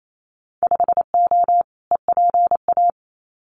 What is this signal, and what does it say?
Morse code, heard as a keyed tone, about 28 wpm, 700 Hz. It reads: 5O EPA
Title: Winter Field Day Exchanges Encore 28wpm